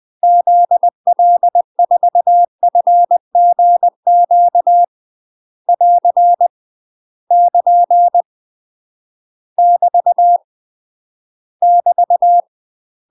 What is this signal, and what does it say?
Morse code, heard as a keyed tone, about 20 wpm, 700 Hz. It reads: ZL4FGQ <AR> <KN> <BT> <BT>
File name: UK MASTER QSOs TRAINING / 20WPM / UK MASTER QSOs 0029